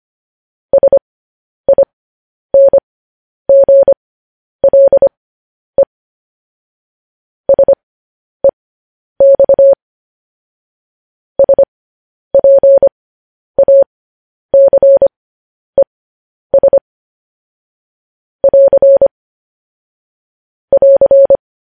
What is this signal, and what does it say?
Morse code, heard as a keyed tone, about 25 wpm, spaced as for 10 wpm, 550 Hz. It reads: SINGLE SEX SPACES <AR> <AR>
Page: news headlines in morse code at 10 wpm